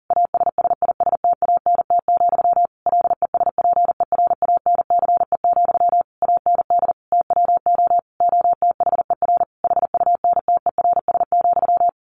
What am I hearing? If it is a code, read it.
Morse code, heard as a keyed tone, about 40 wpm, 700 Hz. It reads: ASSISTANT, LESPERANCE, AND TWO OTHER HUNTERS,
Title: A Sound Of Thunder 40wpm